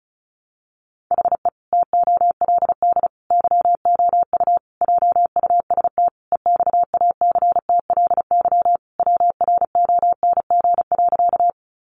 Morse code, heard as a keyed tone, about 34 wpm, 700 Hz. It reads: HE TOLD YOU JUST EXACTLY WRONG.
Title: Sentences from Top 900 Words 35wpm